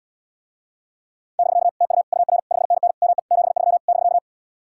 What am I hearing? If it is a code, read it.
Morse code, heard as a keyed tone, about 77 wpm, 700 Hz. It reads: <BK> TU UR 5NN ME 73 <BK>